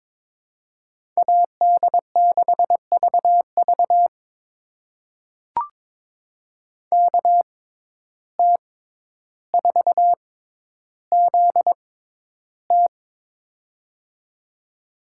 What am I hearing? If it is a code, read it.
Morse code, heard as a keyed tone, about 21 wpm, spaced as for 11 wpm, 700 Hz. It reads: AD6VV KT4ZT
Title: USA Call Signs - 5 Characters - 6x Character Spacing - 22wpm